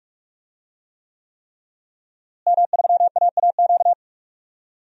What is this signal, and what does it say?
Morse code, heard as a keyed tone, about 42 wpm, 700 Hz. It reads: M3AAQ
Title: DXCC-100 Call Signs - 5 Characters - Repeated 3x - 45wpm